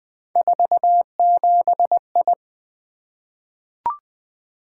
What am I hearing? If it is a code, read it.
Morse code, heard as a keyed tone, about 20 wpm, 700 Hz. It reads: 47I
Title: Three Letter-Number Combinations - 20wpm